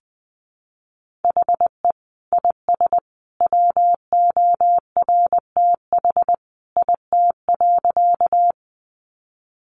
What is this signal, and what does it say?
Morse code, heard as a keyed tone, about 20 wpm, 700 Hz. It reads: HE IS WORTH IT.
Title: Sentences from Top 800 Words 20wpm